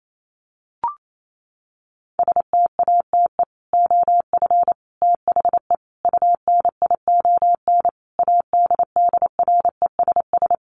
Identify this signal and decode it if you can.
Morse code, heard as a keyed tone, about 28 wpm, 700 Hz. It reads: STATE OF THE UNION ADDRESS
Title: Sets of 5 Words Encore 28wpm